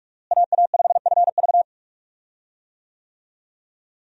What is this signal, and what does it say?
Morse code, heard as a keyed tone, about 46 wpm, 700 Hz. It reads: AA5FV
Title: USA Call Signs - 5 Characters - Repeated 2x - 45wpm